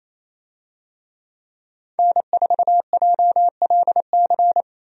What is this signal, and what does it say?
Morse code, heard as a keyed tone, about 28 wpm, 700 Hz. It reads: N4JLC
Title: USA Call Signs - 5 Characters - 28wpm